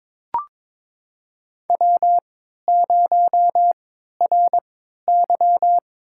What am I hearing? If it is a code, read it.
Morse code, heard as a keyed tone, about 22 wpm, spaced as for 13 wpm, 700 Hz. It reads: W0RY
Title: USA Call Signs - 3x Character Spacing - 22wpm